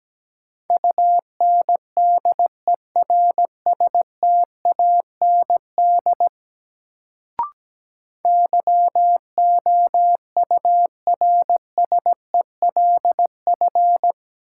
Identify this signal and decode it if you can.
Morse code, heard as a keyed tone, about 17 wpm, 700 Hz. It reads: UNDERSTAND YOURSELF
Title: Sentences from Top 600 Words - New Word Drill - 17wpm